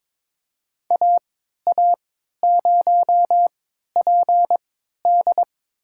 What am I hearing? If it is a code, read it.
Morse code, heard as a keyed tone, about 22 wpm, spaced as for 12 wpm, 700 Hz. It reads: AA0PD
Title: USA Call Signs - 5 Characters - 3x Character Spacing - 22wpm